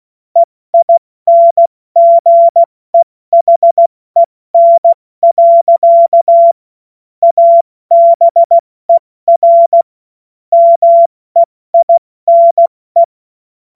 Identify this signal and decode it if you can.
Morse code, heard as a keyed tone, about 16 wpm, 680 Hz. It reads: EINGEHEN. ABER MEINE
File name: Morsekurs / Witze / Witz 19 März 2024 70 F80